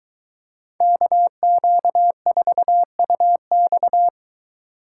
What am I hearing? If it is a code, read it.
Morse code, heard as a keyed tone, about 22 wpm, 700 Hz. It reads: KQ4UX